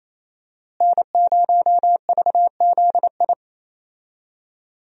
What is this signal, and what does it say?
Morse code, heard as a keyed tone, about 28 wpm, 700 Hz. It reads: N0VZI
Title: USA Call Signs - 5 Characters - Repeated 2x - 28wpm